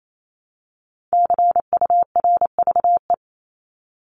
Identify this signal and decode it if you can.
Morse code, heard as a keyed tone, about 28 wpm, 700 Hz. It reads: CURVE
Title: Top 1501-1750 Words 28wpm